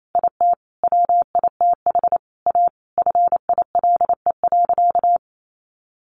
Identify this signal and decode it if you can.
Morse code, heard as a keyed tone, about 28 wpm, 700 Hz. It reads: IT WITH A FILE.
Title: Sentences from Top 1200 Words 28wpm